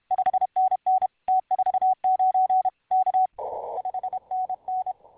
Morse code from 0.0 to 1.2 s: 5NN